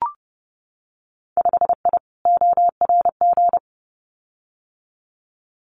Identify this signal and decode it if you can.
Morse code, heard as a keyed tone, about 30 wpm, 700 Hz. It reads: 5I ORG
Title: Winter Field Day Exchanges Encore 30wpm